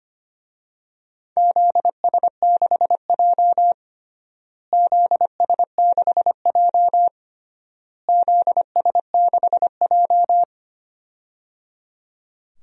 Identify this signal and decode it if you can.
Morse code, heard as a keyed tone, about 25 wpm, 700 Hz. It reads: ZS6J ZS6J ZS6J
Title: DXCC-100 Call Signs - 4 Characters - Repeated 3x - 25wpm